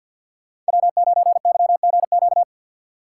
Morse code, heard as a keyed tone, about 61 wpm, 700 Hz. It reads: W9YGQ
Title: Speed Racing - USA 1x3 Call Signs 50wpm